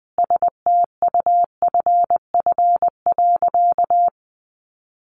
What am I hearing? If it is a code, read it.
Morse code, heard as a keyed tone, about 19 wpm, 700 Hz. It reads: STUFF.
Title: Sentences from Top 1400 Words 20wpm